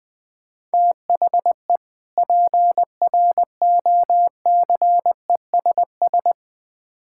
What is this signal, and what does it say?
Morse code, heard as a keyed tone, about 20 wpm, 700 Hz. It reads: THE PROCESS
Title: Build A Phrase - 3-Word Phrases - 20wpm